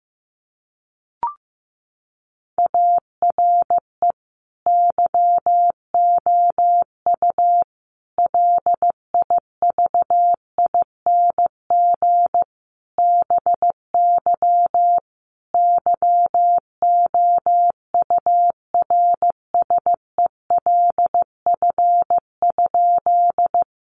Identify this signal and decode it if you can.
Morse code, heard as a keyed tone, about 15 wpm, 700 Hz. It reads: ARE YOU LIVING BY YOURSELF?
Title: Sentences from Top 1100 Words 15wpm